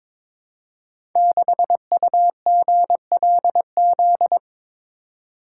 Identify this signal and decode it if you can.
Morse code, heard as a keyed tone, about 22 wpm, 700 Hz. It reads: 6UGLZ
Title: Code Groups - 5 Letter:Numbers - 22wpm